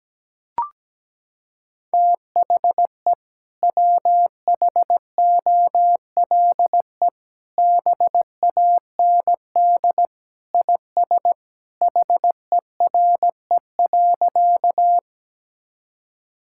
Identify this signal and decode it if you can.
Morse code, heard as a keyed tone, about 17 wpm, 700 Hz. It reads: THE WHOLE BAND IS HERE.
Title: Sentences from Top 1600 Words 17wpm